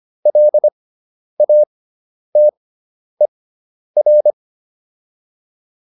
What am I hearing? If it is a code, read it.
Morse code, heard as a keyed tone, about 25 wpm, spaced as for 10 wpm, 600 Hz. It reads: LATER